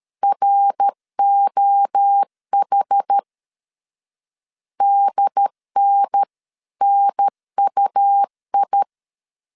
Morse code, heard as a keyed tone, about 13 wpm, 800 Hz. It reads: ROH DNNUI
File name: CW / DET / 1 3 1500C S B 12wpm 2828